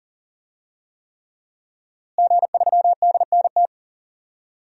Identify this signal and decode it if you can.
Morse code, heard as a keyed tone, about 40 wpm, 700 Hz. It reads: G3DNT
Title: DXCC-100 Call Signs - 5 Characters - Repeated 3x - 40wpm